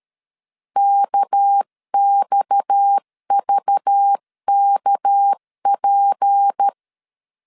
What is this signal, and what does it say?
Morse code, heard as a keyed tone, about 13 wpm, 800 Hz. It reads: KXVKP
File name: CW / DET / 1 6 1500C S C 12wpm 3533